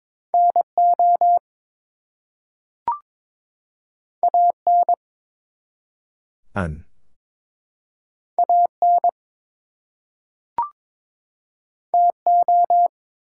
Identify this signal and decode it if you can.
Morse code, heard as a keyed tone, about 22 wpm, 700 Hz. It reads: NO AN AN TO
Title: Two Letter Words - 22wpm